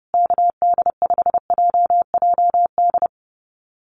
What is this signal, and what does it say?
Morse code, heard as a keyed tone, about 30 wpm, 700 Hz. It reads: KD5JJD
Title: USA Call Signs Repeated 3x 30wpm